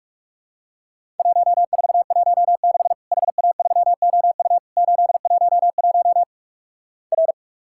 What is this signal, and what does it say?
Morse code, heard as a keyed tone, about 46 wpm, 700 Hz. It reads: 1416 HA3OU 811 R
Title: Worked All Europe DX Contest - QTC Exchange - 45wpm